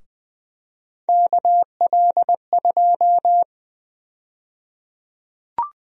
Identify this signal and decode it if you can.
Morse code, heard as a keyed tone, about 20 wpm, 700 Hz. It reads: KL2
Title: USA Call Sign Prefixes - 20wpm